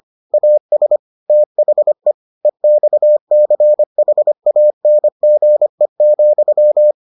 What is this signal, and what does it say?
Morse code, heard as a keyed tone, about 25 wpm, 600 Hz. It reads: AS THE EXCHANGE,